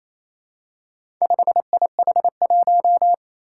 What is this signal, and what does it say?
Morse code, heard as a keyed tone, about 28 wpm, 700 Hz. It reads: 5IH1
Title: Four Challenging Letter-Number Combinations 28wpm